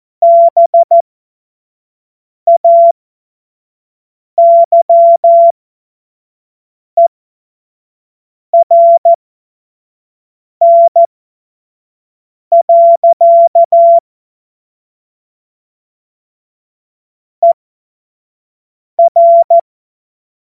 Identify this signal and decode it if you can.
Morse code, heard as a keyed tone, about 14 wpm, spaced as for 5 wpm, 680 Hz. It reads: BAYERN. ER
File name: Morsekurs / Witze / Witz 01. März 2023 25 F70